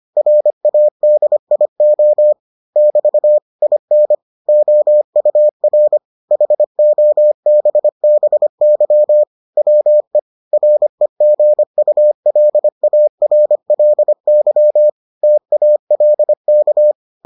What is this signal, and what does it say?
Morse code, heard as a keyed tone, about 25 wpm, 600 Hz. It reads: RADIO <BT> IN OUR HOBBY WE REGULARLY TALK